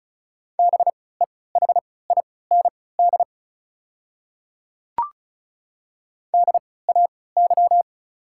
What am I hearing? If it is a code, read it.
Morse code, heard as a keyed tone, about 36 wpm, spaced as for 20 wpm, 700 Hz. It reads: BEHIND DAY
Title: 3x Character Spacing - Top 500 Words - 35wpm